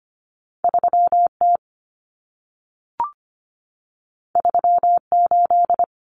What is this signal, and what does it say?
Morse code, heard as a keyed tone, about 25 wpm, 700 Hz. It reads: 3T 38
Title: Common Cut Numbers - 2-Digit 25wpm